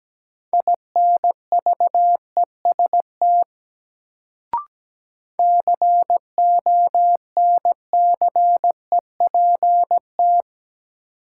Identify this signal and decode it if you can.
Morse code, heard as a keyed tone, about 17 wpm, 700 Hz. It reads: INVEST CONCEPT
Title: Sentences from Top 1100 Words - New Word Drill - 17wpm